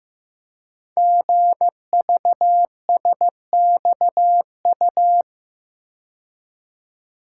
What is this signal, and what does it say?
Morse code, heard as a keyed tone, about 15 wpm, 700 Hz. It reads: GVSXU
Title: Code Groups - 5 Letters - 15wpm